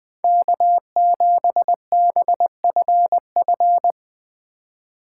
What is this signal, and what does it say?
Morse code, heard as a keyed tone, about 20 wpm, 700 Hz. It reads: K7BFF
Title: USA Call Signs - 4x Character Spacing - 20wpm